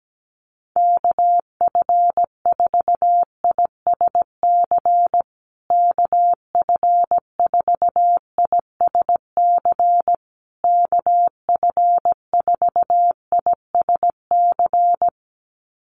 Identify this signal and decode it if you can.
Morse code, heard as a keyed tone, about 17 wpm, 700 Hz. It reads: KF4ISC KF4ISC KF4ISC
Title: USA Call Signs Repeated 3x 17wpm